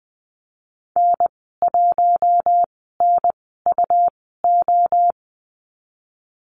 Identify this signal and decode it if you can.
Morse code, heard as a keyed tone, about 20 wpm, spaced as for 15 wpm, 700 Hz. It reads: N1NUO
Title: 122 Lesson 20g - Number 3 - Call Signs 20wpm